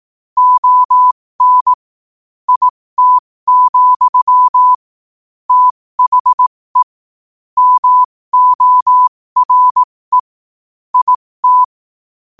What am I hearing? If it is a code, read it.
Morse code, heard as a keyed tone, about 18 wpm, spaced as for 15 wpm, 1000 Hz. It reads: ON IT, THE MORE IT
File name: qotdfiles / 15wpm